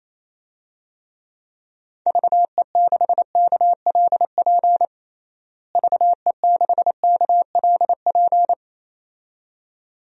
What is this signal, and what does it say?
Morse code, heard as a keyed tone, about 28 wpm, 700 Hz. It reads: VE6KLP VE6KLP
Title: DXCC-100 Call Signs - 6 Characters - Repeated 2x - 28wpm